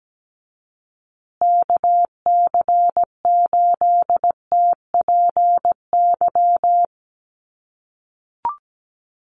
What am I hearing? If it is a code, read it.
Morse code, heard as a keyed tone, about 17 wpm, 700 Hz. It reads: KC8TPY
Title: USA Call Signs Repeated 3x 17wpm